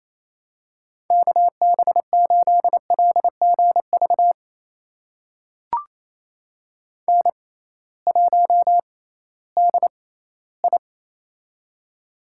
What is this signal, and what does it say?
Morse code, heard as a keyed tone, about 28 wpm, spaced as for 16 wpm, 700 Hz. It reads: KB8LGV N1DI
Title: USA Call Signs - 6x Character Spacing - 28wpm